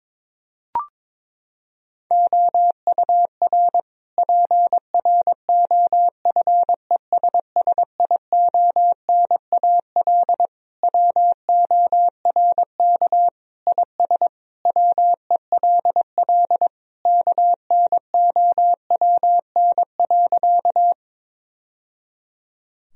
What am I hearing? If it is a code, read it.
Morse code, heard as a keyed tone, about 22 wpm, 700 Hz. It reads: OUR PROFESSIONAL WORK IS WELL KNOWN.
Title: Sentences from Top 1000 Words - Review Part 4 - 22wpm